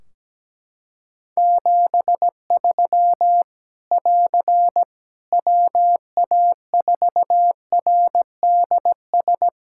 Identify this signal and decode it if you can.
Morse code, heard as a keyed tone, about 17 wpm, 700 Hz. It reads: 73 <AR> WA4RDS
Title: Practice QSO - Exchange 4 - Ending - 17wpm